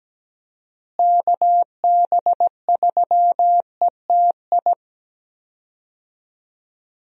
Morse code, heard as a keyed tone, about 17 wpm, 700 Hz. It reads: KB3ETI
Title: USA Call Signs - 6 Characters - Repeated 3x - 17wpm